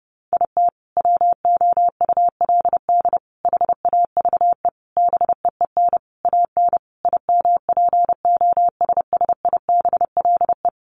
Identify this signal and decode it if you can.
Morse code, heard as a keyed tone, about 30 wpm, 700 Hz. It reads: IT WOULD HAVE BEEN AN IMPOSSIBLE
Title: Sentences from Top 1100 Words 30wpm